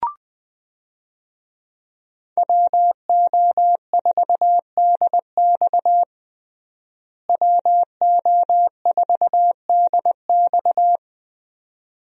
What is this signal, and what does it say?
Morse code, heard as a keyed tone, about 20 wpm, 700 Hz. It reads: WO4DX WO4DX
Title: USA Call Signs - 5 Characters - Repeated 2x - 20wpm